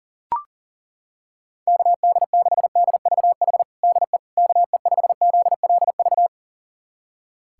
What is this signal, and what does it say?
Morse code, heard as a keyed tone, about 37 wpm, 700 Hz. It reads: KD6DVH DE KE5ZLV
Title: USA Callsign DE Callsign 40wpm